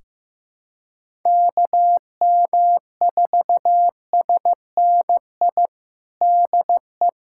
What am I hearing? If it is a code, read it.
Morse code, heard as a keyed tone, about 15 wpm, 700 Hz. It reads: KM4SNI DE
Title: USA Callsign DE Callsign 15wpm